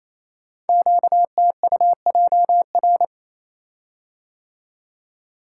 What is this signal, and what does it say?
Morse code, heard as a keyed tone, about 28 wpm, 700 Hz. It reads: QTUJR
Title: Code Groups - 5 Letters - 28wpm